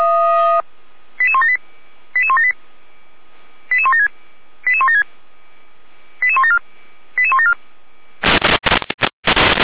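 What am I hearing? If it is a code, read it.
Five-tone selective call (ZVEI), heard as quick runs of five tones, tones of about 70 ms. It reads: 88178 88178 88176 88176 88174 88174
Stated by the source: ZVEI five-tone selective call received on an SDR.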